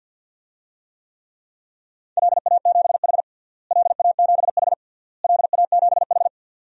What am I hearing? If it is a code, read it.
Morse code, heard as a keyed tone, about 47 wpm, 700 Hz. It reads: LA7H LA7H LA7H
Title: DXCC-100 Call Signs - 4 Characters - Repeated 3x - 50wpm